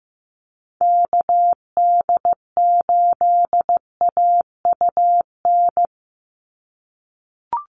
Morse code, heard as a keyed tone, about 15 wpm, 700 Hz. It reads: KD8AUN
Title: USA Call Signs 15wpm